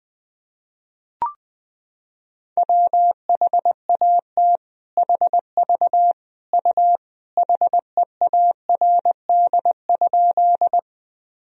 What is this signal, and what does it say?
Morse code, heard as a keyed tone, about 19 wpm, 700 Hz. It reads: WHAT HV U HEARD?
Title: Sentences from Top 400 Words and Abbreviations 20wpm